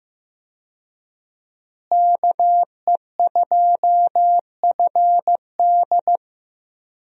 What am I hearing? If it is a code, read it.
Morse code, heard as a keyed tone, about 15 wpm, 700 Hz. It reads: KE2FD
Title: USA Call Signs - 5 Characters - Repeated 2x - 15wpm